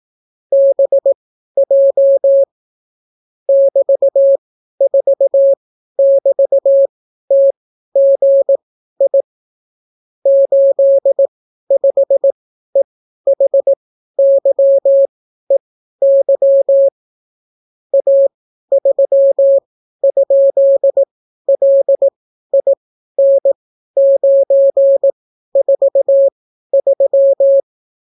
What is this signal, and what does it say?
Morse code, heard as a keyed tone, about 18 wpm, spaced as for 12 wpm, 550 Hz. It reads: BJ <BT>4<BT>TGI 85EHYEY A3?LIN943